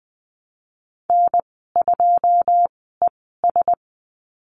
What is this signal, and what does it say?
Morse code, heard as a keyed tone, about 20 wpm, spaced as for 15 wpm, 700 Hz. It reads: N2ES
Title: 186 Lesson 30g - Letter b - Call Signs 20wpm